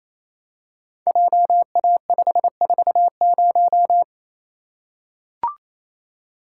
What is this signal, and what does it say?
Morse code, heard as a keyed tone, about 28 wpm, 700 Hz. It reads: JA540
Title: Code Groups - 5 Letter:Numbers - 28wpm